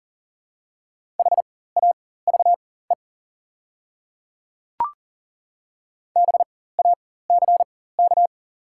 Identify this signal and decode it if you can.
Morse code, heard as a keyed tone, about 40 wpm, spaced as for 19 wpm, 700 Hz. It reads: HAVE BACK
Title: 4x Character Spacing - Top 200 Words - 40wpm